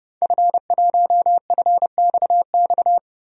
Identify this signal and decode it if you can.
Morse code, heard as a keyed tone, about 30 wpm, 700 Hz. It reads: F1FXX
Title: DXCC-100 Call Signs - 5 Characters - 30wpm